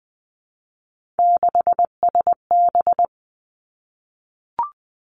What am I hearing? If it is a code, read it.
Morse code, heard as a keyed tone, about 20 wpm, 700 Hz. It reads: 6SB
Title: Three Challenging Letter-Number Combinations 20wpm